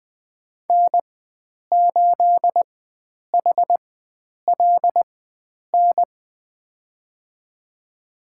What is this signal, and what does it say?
Morse code, heard as a keyed tone, about 20 wpm, spaced as for 9 wpm, 700 Hz. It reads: N8HLN
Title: USA Call Signs - 4x Character Spacing - 20wpm